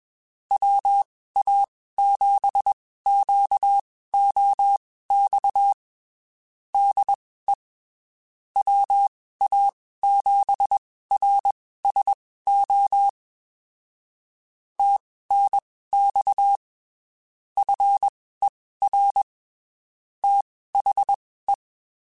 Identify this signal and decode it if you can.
Morse code, heard as a keyed tone, about 21 wpm, spaced as for 15 wpm, 800 Hz. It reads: WA7QOX DE WA7RSO TNX FER THE
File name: (spaced) / QSOs Spaced-mp3s / WA7QOX 2 @20 sp